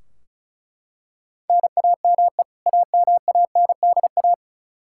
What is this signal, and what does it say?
Morse code, heard as a keyed tone, about 36 wpm, 700 Hz. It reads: NAME AMANDA
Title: Practice QSO - Exchange 1 - Beginning - 35wpm